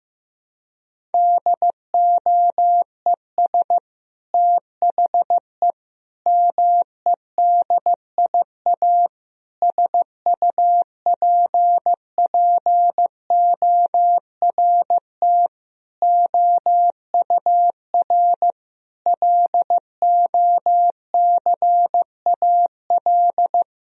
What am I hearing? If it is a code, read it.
Morse code, heard as a keyed tone, about 15 wpm, 700 Hz. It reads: DOES THE MEDIA SUPPORT OUR LOCAL
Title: Sentences from Top 1000 Words - Review Part 4 - 15wpm